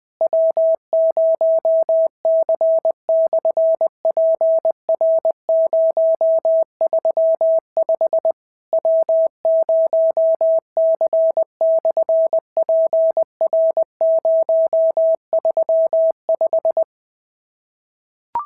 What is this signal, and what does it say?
Morse code, heard as a keyed tone, about 20 wpm, 650 Hz. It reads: W0C/PR035 W0C/PR035
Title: SOTA QSO Practice 20wpm